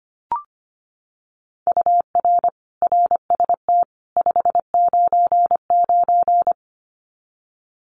Morse code, heard as a keyed tone, about 25 wpm, 700 Hz. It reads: UR RST 599
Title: Common QSO Phrases 25wpm